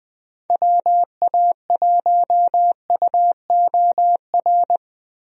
Morse code, heard as a keyed tone, about 20 wpm, 700 Hz. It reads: WA1UOR
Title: USA Call Signs - 6 Characters - Repeated 3x - 20wpm